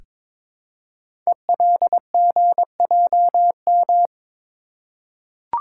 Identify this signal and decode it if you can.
Morse code, heard as a keyed tone, about 22 wpm, 700 Hz. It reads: ELGJM
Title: Code Groups - 5 Letters - 22wpm